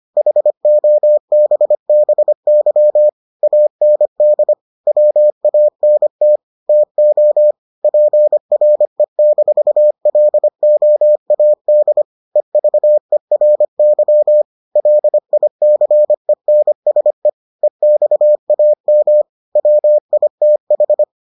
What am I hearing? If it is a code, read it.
Morse code, heard as a keyed tone, about 25 wpm, 600 Hz. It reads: HOBBY AND WANT TO PRE-LOAD EVERY LICENSE EXAM WITH